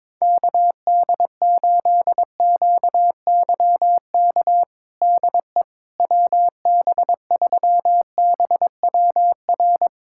Morse code, heard as a keyed tone, about 22 wpm, 700 Hz. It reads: KD8QYK DE WB3BWR
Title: USA Callsign DE Callsign 22wpm